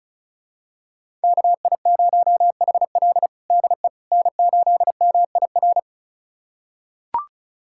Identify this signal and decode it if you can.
Morse code, heard as a keyed tone, about 36 wpm, 700 Hz. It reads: KI0HL DE N8MIR
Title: USA Callsign DE Callsign - 5 Characters - 35wpm